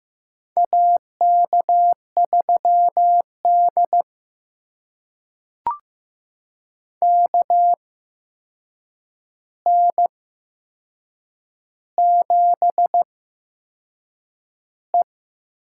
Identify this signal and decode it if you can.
Morse code, heard as a keyed tone, about 15 wpm, spaced as for 6 wpm, 700 Hz. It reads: AK3D KN7E
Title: USA Call Signs - 4 Characters - 8x Character Spacing - 15wpm